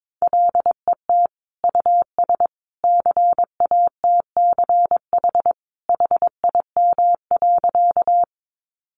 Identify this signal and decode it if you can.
Morse code, heard as a keyed tone, about 22 wpm, 700 Hz. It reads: LET US CATCH HIM.
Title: Sentences from Top 1500 Words 22wpm